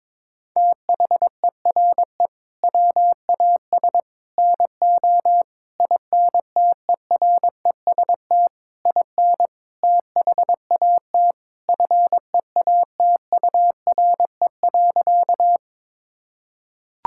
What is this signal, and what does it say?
Morse code, heard as a keyed tone, about 22 wpm, 700 Hz. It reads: THERE WAS NO INTEREST IN THAT FEATURE.
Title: Sentences from Top 1000 Words - Review Part 5 - 22wpm